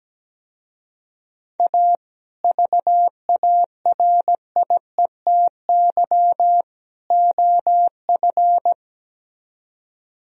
Morse code, heard as a keyed tone, about 18 wpm, 700 Hz. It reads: A VARIETY OF
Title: Speed-Racing - Sets of 3 Words 17wpm